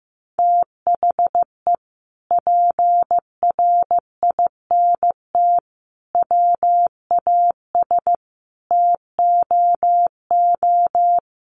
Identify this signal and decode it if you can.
Morse code, heard as a keyed tone, about 14 wpm, 700 Hz. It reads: THE PRINT WAS TOO
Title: Sentences from Top 1400 Words 15wpm